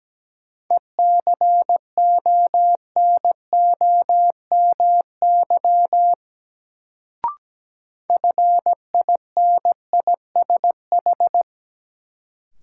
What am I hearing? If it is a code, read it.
Morse code, heard as a keyed tone, about 17 wpm, 700 Hz. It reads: ECONOMY FINISH
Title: Sentences from Top 1000 Words - Review Part 3 - New Word Drill - 17wpm